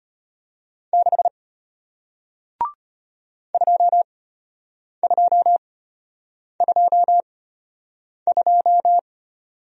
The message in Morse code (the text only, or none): 6 2222